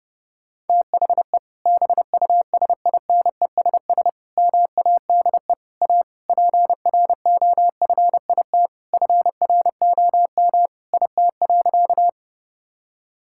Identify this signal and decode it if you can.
Morse code, heard as a keyed tone, about 32 wpm, 700 Hz. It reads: THE BUSINESS MADE A PROFIT FROM IT.
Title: Sentences from Top 1000 Words - Review Part 5 - 30wpm